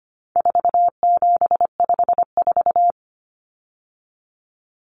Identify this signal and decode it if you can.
Morse code, heard as a keyed tone, about 25 wpm, 700 Hz. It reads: 4754
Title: Common Cut Numbers - 4-Digit 25wpm